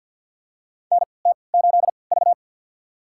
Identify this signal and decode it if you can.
Morse code, heard as a keyed tone, about 46 wpm, spaced as for 27 wpm, 700 Hz. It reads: NT8V
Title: USA Call Signs - 4 Characters - 3x Character Spacing - 50wpm